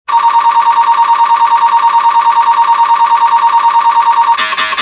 There is no code.